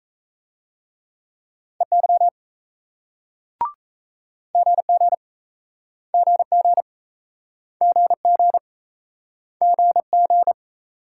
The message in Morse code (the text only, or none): EY GG GG GG GG